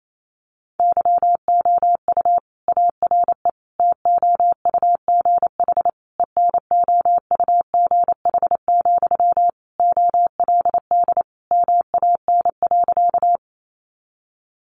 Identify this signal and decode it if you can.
Morse code, heard as a keyed tone, about 28 wpm, 700 Hz. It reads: YOU ARE TOUGH ENOUGH, OLD MAN.
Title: Sentences from Top 1200 Words 28wpm